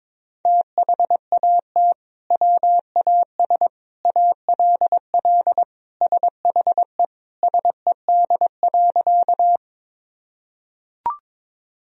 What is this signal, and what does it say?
Morse code, heard as a keyed tone, about 22 wpm, 700 Hz. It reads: THAT WAS ALL SHE SED.
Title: Sentences from Top 100 Words and Abbreviations 22wpm